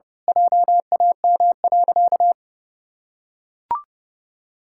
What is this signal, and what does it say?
Morse code, heard as a keyed tone, about 30 wpm, 700 Hz. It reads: JAM.
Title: Pangram Challenge v2 - 30wpm